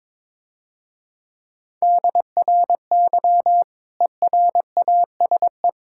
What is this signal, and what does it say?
Morse code, heard as a keyed tone, about 22 wpm, 700 Hz. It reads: DRY ERASE
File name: Easy Words - Speed-Racing with Hints - 22wpm